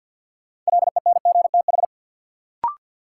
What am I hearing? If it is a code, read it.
Morse code, heard as a keyed tone, about 50 wpm, 700 Hz. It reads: LENGTH
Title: Sentences from Top 1000 Words - New Word Drill - 50wpm